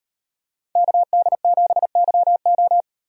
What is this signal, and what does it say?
Morse code, heard as a keyed tone, about 38 wpm, 700 Hz. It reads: KD7YO